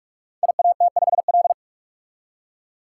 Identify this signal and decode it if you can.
Morse code, heard as a keyed tone, about 43 wpm, 700 Hz. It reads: IAT5L
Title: Code Groups - 5 Letter:Numbers - 45wpm